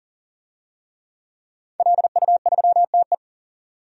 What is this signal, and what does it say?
Morse code, heard as a keyed tone, about 37 wpm, 700 Hz. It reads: LU3TE
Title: DXCC-100 Call Signs - 5 Characters - 40wpm